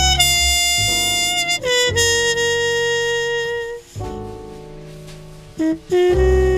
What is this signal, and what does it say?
Dial-up modem handshake.